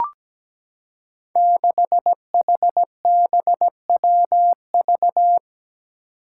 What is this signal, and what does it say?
Morse code, heard as a keyed tone, about 17 wpm, 700 Hz. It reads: 6HBWV
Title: Five Challenging Letter-Number Combinations 17wpm